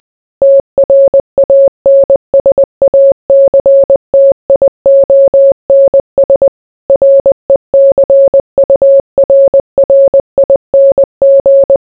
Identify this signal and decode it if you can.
Morse code, heard as a keyed tone, about 20 wpm, 550 Hz. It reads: TRANSACTIONS RECURRING